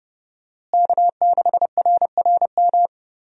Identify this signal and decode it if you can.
Morse code, heard as a keyed tone, about 30 wpm, 700 Hz. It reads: K6RRM
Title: USA Call Signs - 5 Characters - Repeated 2x - 30wpm